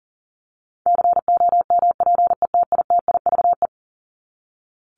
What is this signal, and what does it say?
Morse code, heard as a keyed tone, about 40 wpm, 700 Hz. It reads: COMPETITIVE
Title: Top 1001-1250 Words 40wpm